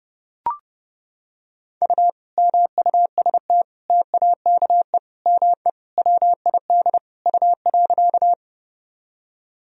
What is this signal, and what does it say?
Morse code, heard as a keyed tone, about 30 wpm, 700 Hz. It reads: U MUST TAKE ME WID U.
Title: Sentences from Top 200 Words and Abbreviations 30wpm